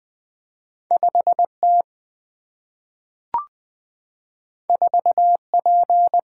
Morse code, heard as a keyed tone, about 20 wpm, 700 Hz. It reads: 5T 4P
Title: Two Letter-Number Combinations - 20wpm